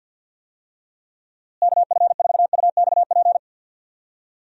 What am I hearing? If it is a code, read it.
Morse code, heard as a keyed tone, about 52 wpm, 700 Hz. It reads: KF4UXP
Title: USA Call Signs - 6 Characters - Repeated 3x - 50wpm